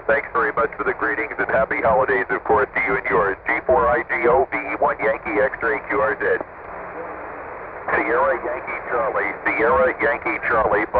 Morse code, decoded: none